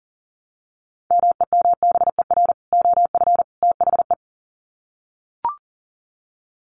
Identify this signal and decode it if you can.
Morse code, heard as a keyed tone, about 38 wpm, 700 Hz. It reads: MEMBER OF THE
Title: Sets of 3 Words 40wpm